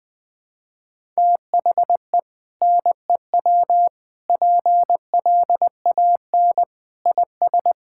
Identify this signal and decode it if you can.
Morse code, heard as a keyed tone, about 20 wpm, 700 Hz. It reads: THE NEW PLAN IS
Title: Sentences with 2-Word Phrases - Spoken Beginning - 20wpm